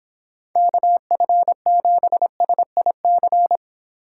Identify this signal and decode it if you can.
Morse code, heard as a keyed tone, about 26 wpm, 700 Hz. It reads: KF7SIC